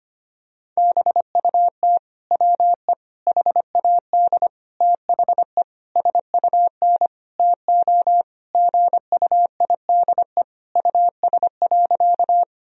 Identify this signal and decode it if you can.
Morse code, heard as a keyed tone, about 25 wpm, 700 Hz. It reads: BUT WE HAD THE SUN TO GUIDE US.
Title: Sentences from Top 1000 Words - Review Part 3 - 25wpm